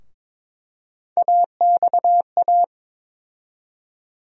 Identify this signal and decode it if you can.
Morse code, heard as a keyed tone, about 22 wpm, 700 Hz. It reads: AXA
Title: USA Call Sign Postfixes - 3 Letters 22wpm